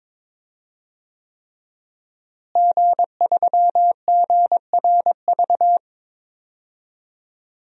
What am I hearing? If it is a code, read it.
Morse code, heard as a keyed tone, about 22 wpm, 700 Hz. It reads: G3GRV
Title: DXCC-100 Call Signs - 5 Characters - Repeated 3x - 22wpm